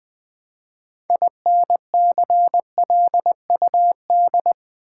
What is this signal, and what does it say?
Morse code, heard as a keyed tone, about 20 wpm, 700 Hz. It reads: INCLUD
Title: Build A Word - 3 Letters At A Time - 8-Letter Words - 20wpm